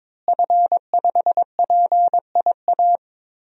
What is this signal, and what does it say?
Morse code, heard as a keyed tone, about 22 wpm, 700 Hz. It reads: F5PIA
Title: DXCC-100 Call Signs - 5 Characters - Repeated 3x - 22wpm